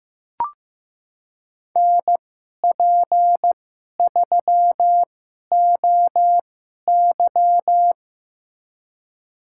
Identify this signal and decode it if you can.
Morse code, heard as a keyed tone, about 15 wpm, spaced as for 11 wpm, 700 Hz. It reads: NP3OY
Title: USA Call Signs - 5 Characters - 2x Character Spacing - 15wpm